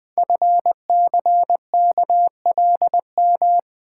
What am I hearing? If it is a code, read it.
Morse code, heard as a keyed tone, about 20 wpm, 700 Hz. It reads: FCKLM
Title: Code Groups - 5 Letters - 20wpm